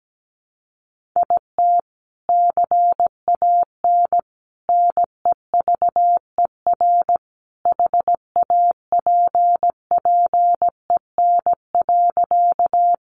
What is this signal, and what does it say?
Morse code, heard as a keyed tone, about 17 wpm, 700 Hz. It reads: IT CAN NEVER HAPPEN.
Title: Sentences from Top 400 Words 17wpm